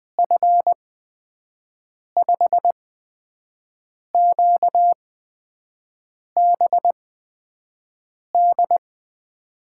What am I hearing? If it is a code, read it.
Morse code, heard as a keyed tone, about 20 wpm, spaced as for 5 wpm, 700 Hz. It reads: F5QBD
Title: USA Call Signs - 6 Characters - 8x Character Spacing - 20wpm